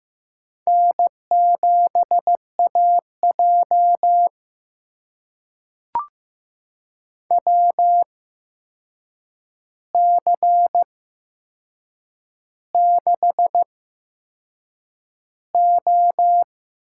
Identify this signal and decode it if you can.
Morse code, heard as a keyed tone, about 15 wpm, spaced as for 6 wpm, 700 Hz. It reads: N7AJ WC6O
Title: USA Call Signs - 4 Characters - 8x Character Spacing - 15wpm